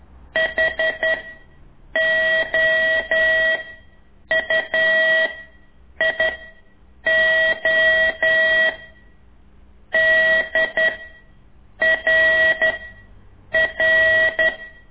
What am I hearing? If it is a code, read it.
Morse code, heard as a keyed tone, about 11 wpm, 1950 Hz. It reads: H O U I O D R R
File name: ik1weg2 / CORSI CW